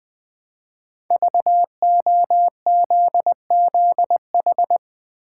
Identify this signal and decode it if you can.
Morse code, heard as a keyed tone, about 20 wpm, 700 Hz. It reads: VOZZH
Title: Code Groups - 5 Letter:Numbers - 20wpm